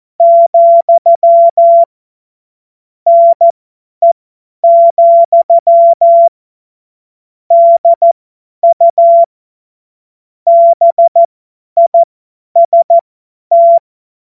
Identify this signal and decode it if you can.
Morse code, heard as a keyed tone, about 14 wpm, spaced as for 10 wpm, 680 Hz. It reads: , NE, DU BIST
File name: Morsekurs / Witze / Witz 28. Dezember 2021, 50 F70